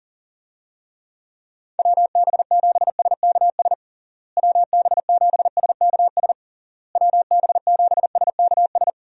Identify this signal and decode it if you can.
Morse code, heard as a keyed tone, about 40 wpm, 700 Hz. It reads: WB7SKS WB7SKS WB7SKS
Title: USA Call Signs - 6 Characters - Repeated 3x - 40wpm